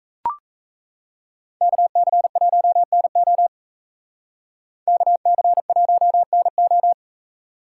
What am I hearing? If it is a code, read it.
Morse code, heard as a keyed tone, about 36 wpm, 700 Hz. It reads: KC1NO KC1NO